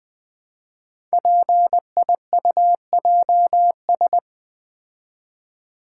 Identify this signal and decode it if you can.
Morse code, heard as a keyed tone, about 20 wpm, 700 Hz. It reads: PIUJS
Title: Code Groups - 5 Letters - 20wpm